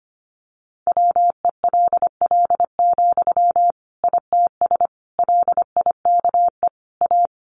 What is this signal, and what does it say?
Morse code, heard as a keyed tone, about 25 wpm, 700 Hz. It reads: WELL, ITS LIKE A
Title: Sentences from Top 600 Words 25wpm